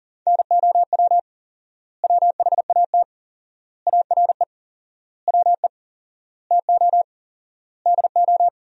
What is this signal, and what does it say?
Morse code, heard as a keyed tone, about 37 wpm, 700 Hz. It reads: NOW WHAT ARE WE TO DO
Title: 4x Word-Spacing - Sentences from Top 100 Words - 40wpm